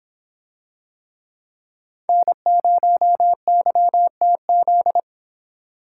Morse code, heard as a keyed tone, about 24 wpm, 700 Hz. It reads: N0YTZ